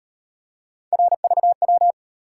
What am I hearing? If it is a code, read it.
Morse code, heard as a keyed tone, about 35 wpm, 700 Hz. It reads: RVW